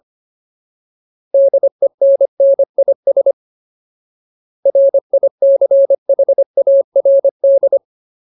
Morse code, heard as a keyed tone, about 25 wpm, 550 Hz. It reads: DENNIS RICHARD
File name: Ultimate Word List / 25WPM Ultimate